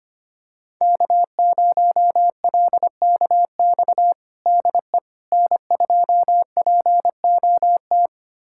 Morse code, heard as a keyed tone, about 25 wpm, 700 Hz. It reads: K0LKX DE N2POT
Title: USA Callsign DE Callsign - 5 Characters - 25wpm